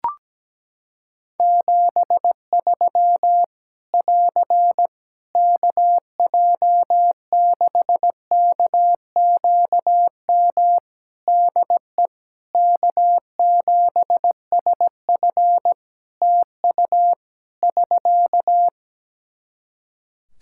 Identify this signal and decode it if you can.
Morse code, heard as a keyed tone, about 17 wpm, 700 Hz. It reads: 73 <AR> KJ6KQM DE K7SF TU <SK>
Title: Practice QSO - Exchange 4 - Ending - 17wpm